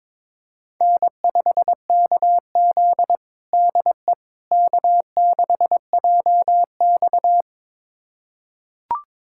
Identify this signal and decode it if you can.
Morse code, heard as a keyed tone, about 22 wpm, 700 Hz. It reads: N5KZ DE K6JX
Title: USA Callsign DE Callsign - 4 Characters - 22wpm